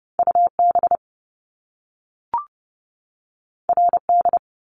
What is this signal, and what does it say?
Morse code, heard as a keyed tone, about 30 wpm, 700 Hz. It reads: UB RD
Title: Top Two Letter Combinations in Words 30wpm